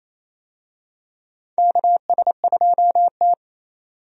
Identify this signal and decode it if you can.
Morse code, heard as a keyed tone, about 28 wpm, 700 Hz. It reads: KS2T
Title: USA Call Signs - 4 Characters - 28wpm